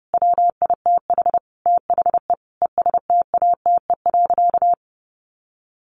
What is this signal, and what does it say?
Morse code, heard as a keyed tone, about 30 wpm, 700 Hz. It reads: WITH THE ESTATE.
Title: Sentences from Top 1200 Words 30wpm